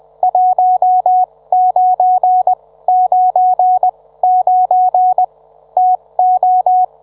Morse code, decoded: 1999 TO